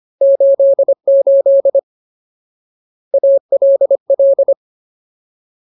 Morse code, heard as a keyed tone, about 25 wpm, 550 Hz. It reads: 88 ALL
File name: Ultimate Word List / 25WPM Ultimate